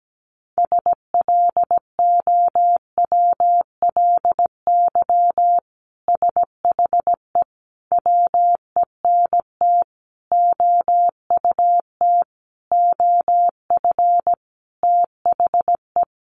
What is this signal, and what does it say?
Morse code, heard as a keyed tone, about 17 wpm, 700 Hz. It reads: SLOWLY SHE WENT OUT OF THE
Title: Sentences from Top 1100 Words 17wpm